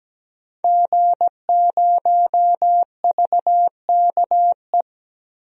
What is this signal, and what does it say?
Morse code, heard as a keyed tone, about 17 wpm, 700 Hz. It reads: G0VKE
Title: DXCC-100 Call Signs - 5 Characters - Repeated 3x - 17wpm